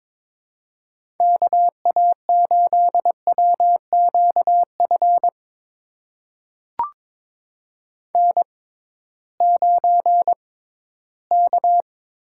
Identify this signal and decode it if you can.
Morse code, heard as a keyed tone, about 22 wpm, spaced as for 14 wpm, 700 Hz. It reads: KA8WQF N9K
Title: USA Call Signs - 6x Character Spacing - 22wpm